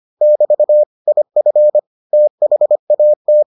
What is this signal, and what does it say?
Morse code, heard as a keyed tone, about 25 wpm, 600 Hz. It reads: <BT> IF THAT